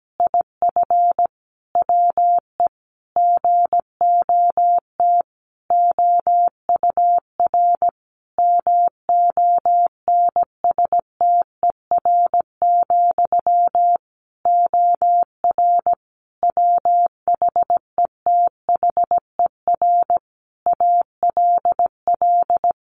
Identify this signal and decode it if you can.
Morse code, heard as a keyed tone, about 17 wpm, 700 Hz. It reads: IF WE GOT OUR MONSTER, OR WHETHER ALL
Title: A Sound Of Thunder 17wpm